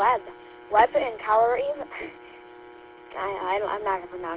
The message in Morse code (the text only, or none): none